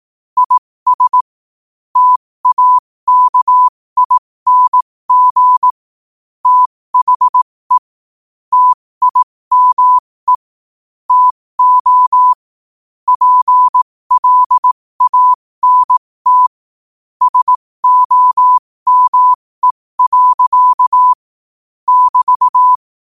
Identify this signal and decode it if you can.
Morse code, heard as a keyed tone, about 18 wpm, spaced as for 15 wpm, 1000 Hz. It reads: IS TAKING THE TIME TO PLANT SOME. <BT>
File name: qotdfiles / 15wpm